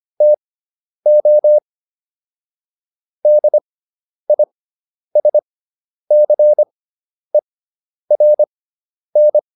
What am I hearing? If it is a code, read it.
Morse code, heard as a keyed tone, about 25 wpm, spaced as for 10 wpm, 600 Hz. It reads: TO DISCERN